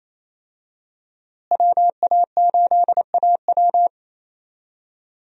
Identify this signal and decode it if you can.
Morse code, heard as a keyed tone, about 28 wpm, 700 Hz. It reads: WA8AW
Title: USA Call Signs - 5 Characters - Repeated 2x - 28wpm